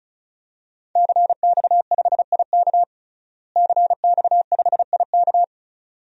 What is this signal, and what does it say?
Morse code, heard as a keyed tone, about 34 wpm, 700 Hz. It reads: CX5IK CX5IK
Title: DXCC-100 Call Signs - 5 Characters - Repeated 2x - 35wpm